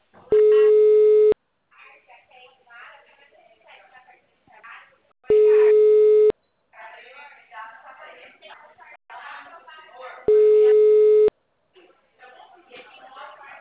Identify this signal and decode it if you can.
Ringback tone (Europe, 425 Hz), with speech.
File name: out- -2019- - - .672